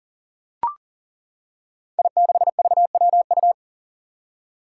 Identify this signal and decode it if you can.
Morse code, heard as a keyed tone, about 39 wpm, 700 Hz. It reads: I6VWU
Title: Five Challenging Letter-Number Combinations 40wpm